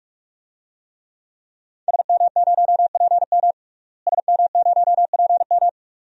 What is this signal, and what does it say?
Morse code, heard as a keyed tone, about 42 wpm, 700 Hz. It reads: SM0PM SM0PM
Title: DXCC-100 Call Signs - 5 Characters - Repeated 2x - 45wpm